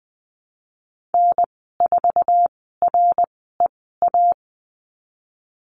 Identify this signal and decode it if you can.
Morse code, heard as a keyed tone, about 20 wpm, 700 Hz. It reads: N 4 R E A
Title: 076 Lesson 13g - Letter L - Call Signs 20wpm